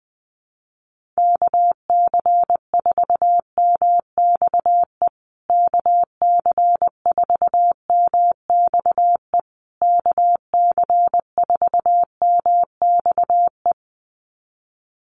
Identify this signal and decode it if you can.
Morse code, heard as a keyed tone, about 20 wpm, 700 Hz. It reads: KC4MXE KC4MXE KC4MXE
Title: USA Call Signs Repeated 3x 20wpm